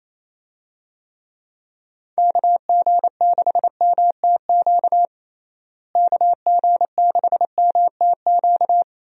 Morse code, heard as a keyed tone, about 28 wpm, 700 Hz. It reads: KG6MTQ KG6MTQ
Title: USA Call Signs - 6 Characters - Repeated 3x - 28wpm